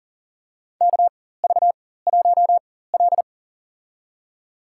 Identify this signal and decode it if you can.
Morse code, heard as a keyed tone, about 40 wpm, spaced as for 19 wpm, 700 Hz. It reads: KV1L
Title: USA Call Signs - 4 Characters - 4x Character Spacing - 40wpm